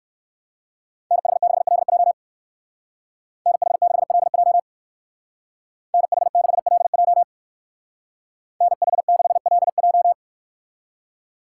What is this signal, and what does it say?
Morse code, heard as a keyed tone, about 56 wpm, 700 Hz. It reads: NH6LJ NH6LJ NH6LJ NH6LJ
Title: Speed Racing - USA 2x2 Call Signs 45wpm